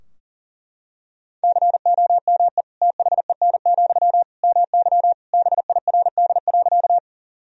Morse code, heard as a keyed tone, about 40 wpm, 700 Hz. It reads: COME THEN, MY BIRD.
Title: Sentences from Top 1000 Words - Review Part 2 - 40wpm